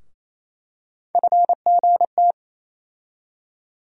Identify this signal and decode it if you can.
Morse code, heard as a keyed tone, about 28 wpm, 700 Hz. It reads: FGT
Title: USA Call Sign Postfixes - 3 Letters 28wpm